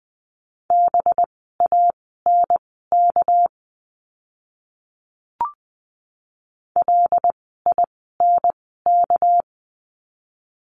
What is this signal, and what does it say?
Morse code, heard as a keyed tone, about 20 wpm, spaced as for 15 wpm, 700 Hz. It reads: BANK LINK
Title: 199 Lesson 32f - Letter k - Words 20wpm